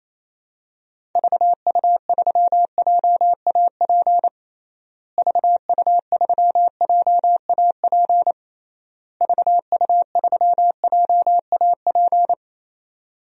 Morse code, heard as a keyed tone, about 28 wpm, 700 Hz. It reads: VU3JAP VU3JAP VU3JAP
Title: DXCC-100 Call Signs - 6 Characters - Repeated 3x - 28wpm